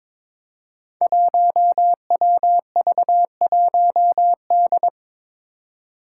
Morse code, heard as a keyed tone, about 22 wpm, 700 Hz. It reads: 1WV1D
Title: Five Challenging Letter-Number Combinations 22wpm